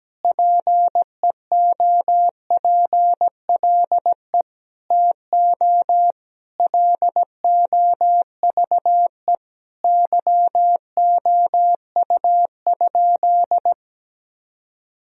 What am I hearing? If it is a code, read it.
Morse code, heard as a keyed tone, about 17 wpm, 700 Hz. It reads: PEOPLE TO LOVE YOU?
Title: Sentences from Top 1000 Words - Review Part 3 - 17wpm